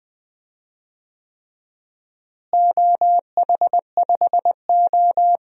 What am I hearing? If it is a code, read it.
Morse code, heard as a keyed tone, about 20 wpm, 700 Hz. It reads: OH5O
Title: DXCC-100 Call Signs - 4 Characters - Repeated 3x - 20wpm